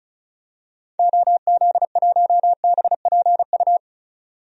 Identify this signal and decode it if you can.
Morse code, heard as a keyed tone, about 36 wpm, 700 Hz. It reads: OZ1BPU
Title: DXCC-100 Call Signs - 6 Characters - 35wpm